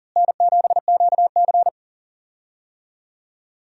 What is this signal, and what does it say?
Morse code, heard as a keyed tone, about 40 wpm, 700 Hz. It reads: N7QC
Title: USA Call Signs - 4 Characters - Repeated 2x - 40wpm